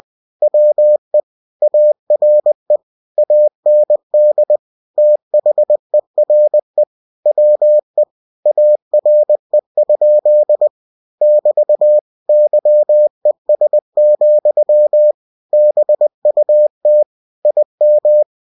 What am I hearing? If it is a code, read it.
Morse code, heard as a keyed tone, about 20 wpm, 600 Hz. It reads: WE ARE AND THERE WE ARE? <BT> YES, BUT IM
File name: cw-books / winnie-the-pooh / winnie-20wpm-page-0055